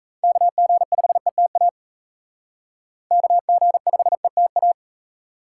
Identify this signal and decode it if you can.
Morse code, heard as a keyed tone, about 34 wpm, 700 Hz. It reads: KG5ETA KG5ETA
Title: Speed Racing - USA Call Signs 28wpm